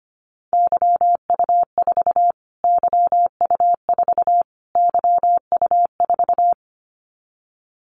Morse code, heard as a keyed tone, about 25 wpm, 700 Hz. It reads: YU4 YU4 YU4
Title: Callsign Prefixes for Top 25 DXCC - Repeated 3x - 25wpm